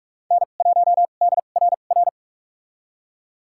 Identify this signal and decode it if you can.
Morse code, heard as a keyed tone, about 46 wpm, spaced as for 33 wpm, 700 Hz. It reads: N1DRR
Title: USA Call Signs - 5 Characters - 2x Character Spacing - 45wpm